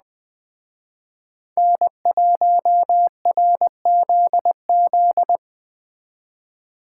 Morse code, heard as a keyed tone, about 20 wpm, 700 Hz. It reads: N1RZZ